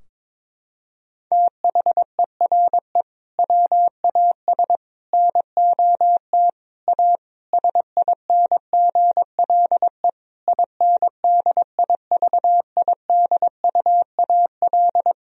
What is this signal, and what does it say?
Morse code, heard as a keyed tone, about 22 wpm, 700 Hz. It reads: THERE WAS NOT A SINGLE INDIVIDUAL
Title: Sentences from Top 1000 Words - Review Part 4 - 22wpm